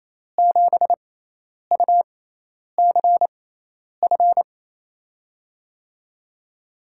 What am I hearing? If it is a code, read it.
Morse code, heard as a keyed tone, about 28 wpm, spaced as for 10 wpm, 700 Hz. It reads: 7UCF
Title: USA Call Signs - 6 Characters - 6x Character Spacing - 28wpm